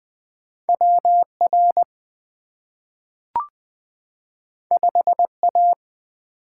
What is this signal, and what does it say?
Morse code, heard as a keyed tone, about 20 wpm, 700 Hz. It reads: W R 5 A
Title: Two Letter-Number Combinations - 20wpm